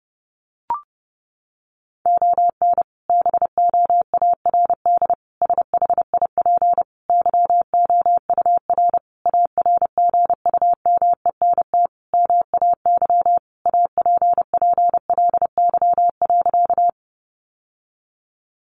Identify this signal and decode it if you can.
Morse code, heard as a keyed tone, about 30 wpm, 700 Hz. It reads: ON BOARD SHIP YOUR ARGUMENT MAY APPLY.
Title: Sentences from Top 1400 Words 30wpm